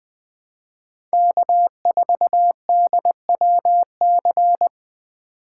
Morse code, heard as a keyed tone, about 20 wpm, 700 Hz. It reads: K4DWC
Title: USA Call Signs - 5 Characters - 3x Character Spacing - 20wpm